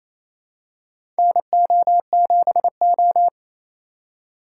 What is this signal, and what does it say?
Morse code, heard as a keyed tone, about 28 wpm, 700 Hz. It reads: NO7O
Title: USA Call Signs - 4x Character Spacing - 28wpm